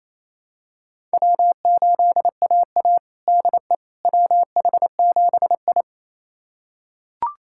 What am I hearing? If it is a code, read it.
Morse code, heard as a keyed tone, about 28 wpm, 700 Hz. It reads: W8AA DE WH7I
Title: USA Callsign DE Callsign - 4 Characters - 28wpm